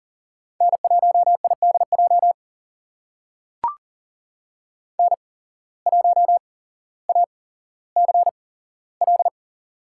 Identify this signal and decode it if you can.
Morse code, heard as a keyed tone, about 40 wpm, spaced as for 17 wpm, 700 Hz. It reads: N1IDJ N1ACL
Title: USA Call Signs - 8x Character Spacing - 40wpm